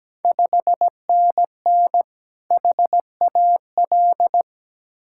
Morse code, heard as a keyed tone, about 17 wpm, 700 Hz. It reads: 5NN HAL
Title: Ontario QSO Party Exchange 17wpm